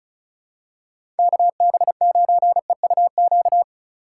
Morse code, heard as a keyed tone, about 36 wpm, 700 Hz. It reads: KB9EUQ
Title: USA Call Signs - 2x Character Spacing - 35wpm